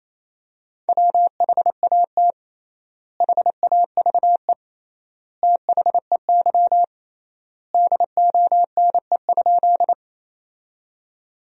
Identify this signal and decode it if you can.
Morse code, heard as a keyed tone, about 28 wpm, 700 Hz. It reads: WHAT HAVE THEY DONE?
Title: 3x Word-Spacing - Sentences from Top 500 Words - 28wpm